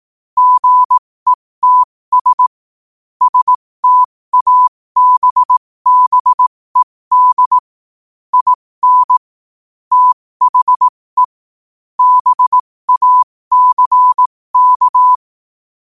Morse code, heard as a keyed tone, about 18 wpm, spaced as for 15 wpm, 1000 Hz. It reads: GETS STABBED IN THE BACK